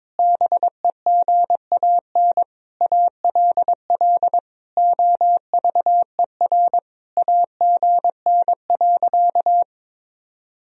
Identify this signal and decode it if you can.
Morse code, heard as a keyed tone, about 22 wpm, 700 Hz. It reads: BEGAN ALL OVER AGN.
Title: Sentences from Top 500 Words and Abbreviations 22wpm